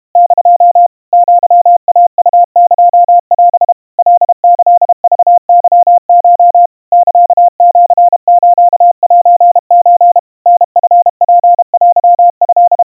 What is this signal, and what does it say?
Morse code (Japanese Wabun code), heard as a keyed tone, about 32 wpm, 700 Hz. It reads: <DO> アイウエオ カキクケコ サシスセソ タチツテト
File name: 110 cpm 700 Hz Crispy